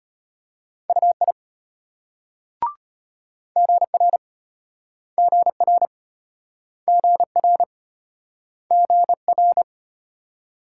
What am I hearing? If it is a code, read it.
Morse code, heard as a keyed tone, about 35 wpm, 700 Hz. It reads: UI GR GR GR GR